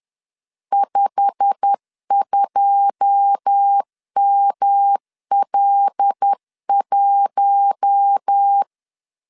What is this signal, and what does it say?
Morse code, heard as a keyed tone, about 10 wpm, 800 Hz. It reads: 52ML1
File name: LEZIONI / 9 2 10wpm 2736